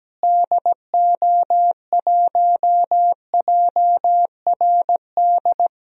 Morse code, heard as a keyed tone, about 17 wpm, 700 Hz. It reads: DO1JRD
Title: DXCC-100 Call Signs - 6 Characters - Repeated 3x - 17wpm